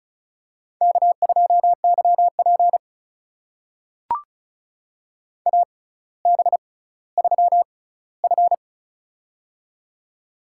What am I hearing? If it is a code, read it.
Morse code, heard as a keyed tone, about 36 wpm, spaced as for 18 wpm, 700 Hz. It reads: K2YP AB3F
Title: USA Call Signs - 4 Characters - 6x Character Spacing - 35wpm